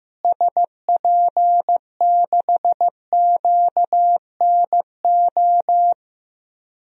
Morse code, heard as a keyed tone, about 15 wpm, 700 Hz. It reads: SP6QNO
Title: DXCC-100 Call Signs - 6 Characters - Repeated 3x - 15wpm